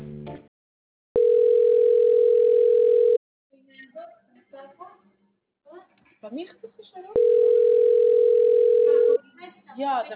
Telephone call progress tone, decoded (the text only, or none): ringback tone